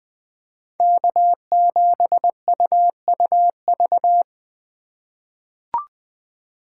Morse code, heard as a keyed tone, about 21 wpm, 700 Hz. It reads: K7UUV
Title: USA Call Signs - 5 Characters - 4x Character Spacing - 20wpm